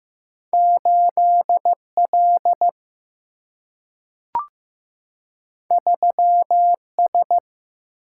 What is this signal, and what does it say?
Morse code, heard as a keyed tone, about 15 wpm, 700 Hz. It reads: 8L 3S
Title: Single-Letter Single-Number 15wpm